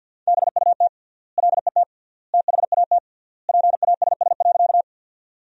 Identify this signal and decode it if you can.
Morse code, heard as a keyed tone, about 50 wpm, 700 Hz. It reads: BUT LET THAT PASS.
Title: 3x Word-Spacing - Sentences from Top 400 Words - 50wpm